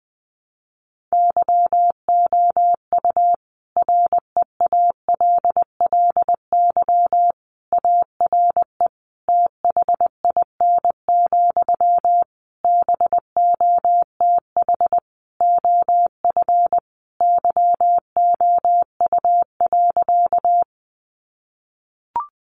Morse code, heard as a keyed tone, about 20 wpm, 700 Hz. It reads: YOU REALLY ARE THIN, BOTH OF YOU.
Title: Sentences from Top 1200 Words 20wpm